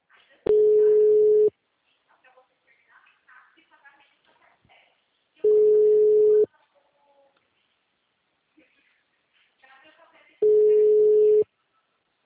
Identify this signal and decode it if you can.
Ringback tone (Europe, 425 Hz), with speech.